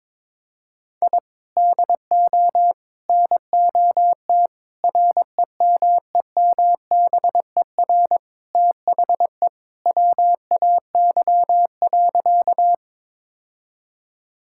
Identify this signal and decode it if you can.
Morse code, heard as a keyed tone, about 22 wpm, 700 Hz. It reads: I DO NOT REMEMBER THE WAY.
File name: practice / 22wpm / Sentences with 4-Word Phrases - 22wpm